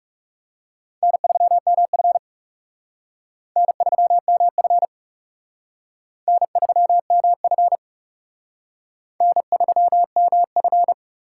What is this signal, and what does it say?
Morse code, heard as a keyed tone, about 36 wpm, 700 Hz. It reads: N3MF N3MF N3MF N3MF
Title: Speed Racing - USA 1x2 Call Signs 30wpm